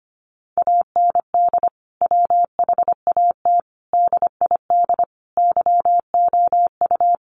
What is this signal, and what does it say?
Morse code, heard as a keyed tone, about 25 wpm, 700 Hz. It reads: AND WHAT DID YOU
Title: Sentences from Top 100 Words 25wpm